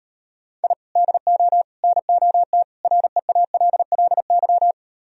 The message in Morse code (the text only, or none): I DO NOT REALLY